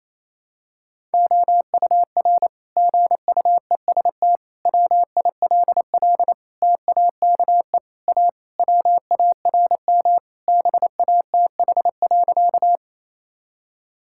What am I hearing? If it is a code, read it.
Morse code, heard as a keyed tone, about 27 wpm, 700 Hz. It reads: OUR GUEST WILL TAKE A WARM BATH.
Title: Sentences from Top 1600 Words 28wpm